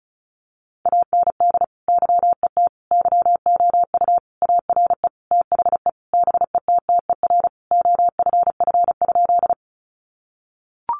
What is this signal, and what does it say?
Morse code, heard as a keyed tone, about 34 wpm, 700 Hz. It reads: AND YET YOU ARE THE BETTER OFF?
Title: Sentences from Top 500 Words 35wpm